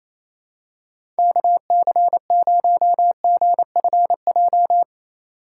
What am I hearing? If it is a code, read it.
Morse code, heard as a keyed tone, about 28 wpm, 700 Hz. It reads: KC0GFJ
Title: USA Call Signs - 6 Characters - 4x Character Spacing - 28wpm